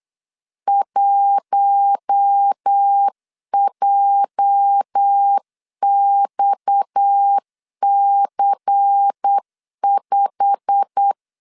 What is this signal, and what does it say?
Morse code, heard as a keyed tone, about 8 wpm, 800 Hz. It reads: 1JXC5
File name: LEZIONI / 9 28 8wpm 3512